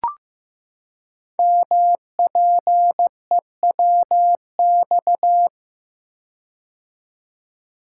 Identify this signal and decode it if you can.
Morse code, heard as a keyed tone, about 15 wpm, 700 Hz. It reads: MPEWX
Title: Code Groups - 5 Letters - 15wpm